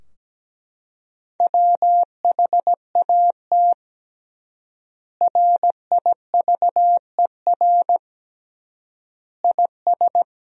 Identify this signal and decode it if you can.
Morse code, heard as a keyed tone, about 17 wpm, 700 Hz. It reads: WHAT RIVER IS
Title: 3x Word-Spacing - Sentences from Top 300 Words - 17wpm